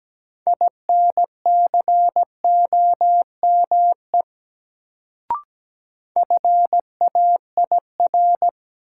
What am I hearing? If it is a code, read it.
Morse code, heard as a keyed tone, about 17 wpm, 700 Hz. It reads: INCOME FAIR
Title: Sentences from Top 1000 Words - Review Part 5 - New Word Drill - 17wpm